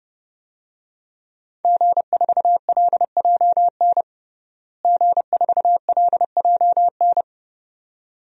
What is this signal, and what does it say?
Morse code, heard as a keyed tone, about 30 wpm, 700 Hz. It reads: G4LJN G4LJN
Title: DXCC-100 Call Signs - 5 Characters - Repeated 2x - 30wpm